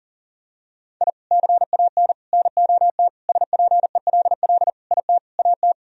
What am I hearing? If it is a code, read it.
Morse code, heard as a keyed tone, about 40 wpm, 700 Hz. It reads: I CAN NOT SPELL IT AT
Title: Sentences from Top 1000 Words - Review Part 1 - 40wpm